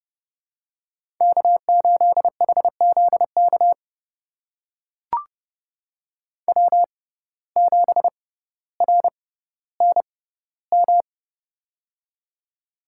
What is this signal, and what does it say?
Morse code, heard as a keyed tone, about 30 wpm, spaced as for 15 wpm, 700 Hz. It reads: K8HZK W7RNM
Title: USA Call Signs - 5 Characters - 6x Character Spacing - 30wpm